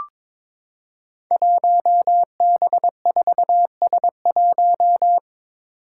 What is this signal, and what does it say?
Morse code, heard as a keyed tone, about 22 wpm, 700 Hz. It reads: 1B4S1
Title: Five Challenging Letter-Number Combinations 22wpm